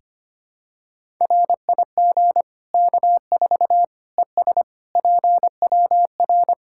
Morse code, heard as a keyed tone, about 25 wpm, 700 Hz. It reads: RIG K4 ES PWR
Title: K1USN SST QSO Practice 25wpm